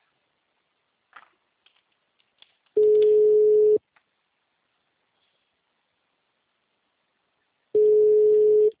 Ringback tone (Europe, 425 Hz).